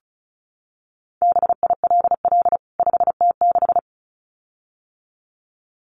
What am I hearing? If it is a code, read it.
Morse code, heard as a keyed tone, about 34 wpm, 700 Hz. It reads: BILL 5T6
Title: CWT Exchange 35wpm